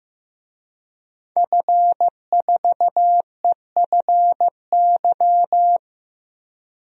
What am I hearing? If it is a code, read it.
Morse code, heard as a keyed tone, about 16 wpm, 700 Hz. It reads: F4EFY
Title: DXCC-100 Call Signs - 5 Characters - Repeated 3x - 15wpm